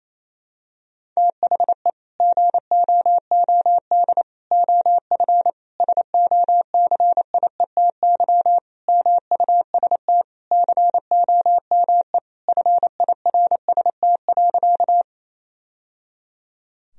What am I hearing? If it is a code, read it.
Morse code, heard as a keyed tone, about 28 wpm, 700 Hz. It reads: THE GOOD OF SOCIETY MUST COME FIRST.
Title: Sentences from Top 1000 Words - Review Part 4 - 28wpm